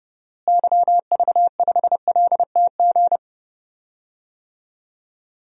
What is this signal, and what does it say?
Morse code, heard as a keyed tone, about 31 wpm, 700 Hz. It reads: YV5LTG
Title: DXCC-100 Call Signs - 6 Characters - Repeated 3x - 30wpm